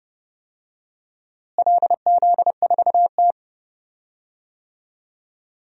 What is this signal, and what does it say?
Morse code, heard as a keyed tone, about 30 wpm, 700 Hz. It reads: LZ4T
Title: DXCC-100 Call Signs - 4 Characters - Repeated 3x - 30wpm